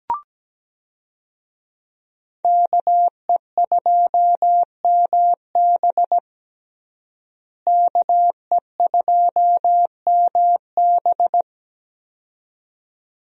Morse code, heard as a keyed tone, about 17 wpm, 700 Hz. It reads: KE2MB KE2MB
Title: USA Call Signs - 5 Characters - Repeated 2x - 17wpm